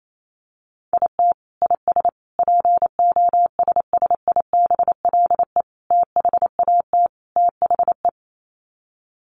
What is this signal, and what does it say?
Morse code, heard as a keyed tone, about 28 wpm, 700 Hz. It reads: IT IS POSSIBLE THAT THE
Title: Sets of 5 Words Encore2 28wpm